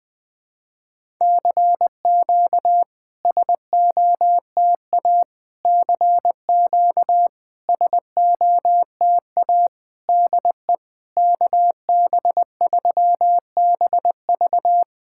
Morse code, heard as a keyed tone, about 20 wpm, 700 Hz. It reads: CQ SOTA CQ SOTA DE KB3BV
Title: SOTA QSO Practice 20wpm